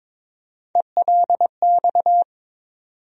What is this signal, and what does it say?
Morse code, heard as a keyed tone, about 22 wpm, 700 Hz. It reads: ELX
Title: USA Call Sign Postfixes - 3 Letters 22wpm